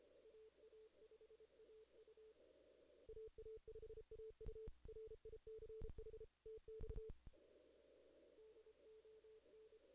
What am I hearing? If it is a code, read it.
Morse code, heard as a keyed tone, about 32 wpm, 440 Hz. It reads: AA5AU AA5AU RICH TX DON
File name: 2016 JAN CW NAQP / 2016 Jan naqp k5ot - 0-11